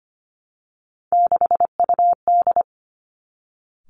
Morse code, heard as a keyed tone, about 25 wpm, 700 Hz. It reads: 6UD
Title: Three Challenging Letter-Number Combinations 25wpm